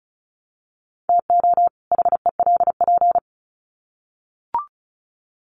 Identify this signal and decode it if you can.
Morse code, heard as a keyed tone, about 34 wpm, 700 Hz. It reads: TO HELP
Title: Sets of 2 Words 35wpm